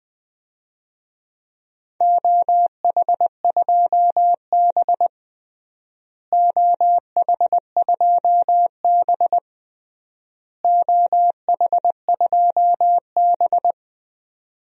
Morse code, heard as a keyed tone, about 20 wpm, 700 Hz. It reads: OH2B OH2B OH2B
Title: DXCC-100 Call Signs - 4 Characters - Repeated 3x - 20wpm